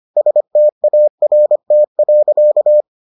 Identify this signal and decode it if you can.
Morse code, heard as a keyed tone, about 25 wpm, 600 Hz. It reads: START.